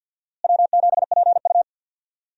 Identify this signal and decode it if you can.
Morse code, heard as a keyed tone, about 50 wpm, 700 Hz. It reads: W7PU
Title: USA Call Signs - 4 Characters - Repeated 2x - 50wpm